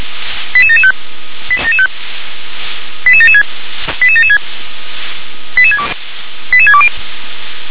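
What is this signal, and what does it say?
Five-tone selective call (ZVEI), heard as quick runs of five tones, tones of about 70 ms. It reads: 88775 88776 88776 88620
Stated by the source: ZVEI five-tone selective call received by SDR.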